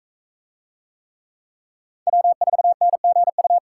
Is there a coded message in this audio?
Yes: Morse code from 2.1 to 3.6 s, heard as a keyed tone, about 42 wpm, 700 Hz. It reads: W4NGU